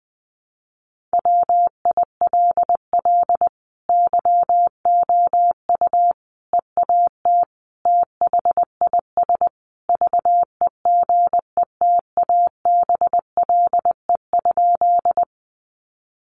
Morse code, heard as a keyed tone, about 20 wpm, 700 Hz. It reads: WILL YOU EAT THIS VEGETABLE?
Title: Sentences from Top 1300 Words 20wpm